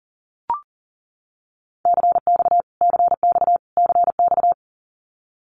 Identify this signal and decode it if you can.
Morse code, heard as a keyed tone, about 40 wpm, 700 Hz. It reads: CX CX CX
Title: Abbreviations Repeated 3x - 40wpm